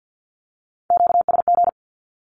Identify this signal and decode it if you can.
Morse code, heard as a keyed tone, about 50 wpm, 700 Hz. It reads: QSG
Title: Q-Codes with Spoken Meaning - 50wpm